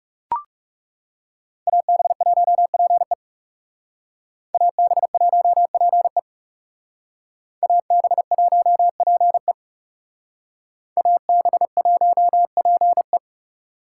Morse code, heard as a keyed tone, about 39 wpm, 700 Hz. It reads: AB1PE AB1PE AB1PE AB1PE